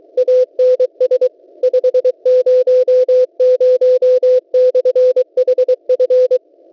Morse code, heard as a keyed tone, about 23 wpm, 510 Hz. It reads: ANS 500/HF
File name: WebSDR records / w7pua-2.ddns.net 12808.50 cw